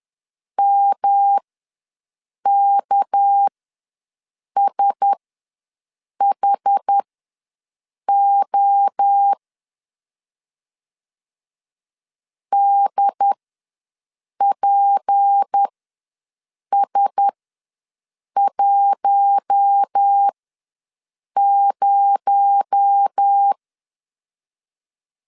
Morse code, heard as a keyed tone, about 10 wpm, spaced as for 6 wpm, 800 Hz. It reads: MKSHO DPS10
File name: LEZIONI / 6 PKX0 / 6 PKX0 ws 10wpm 4220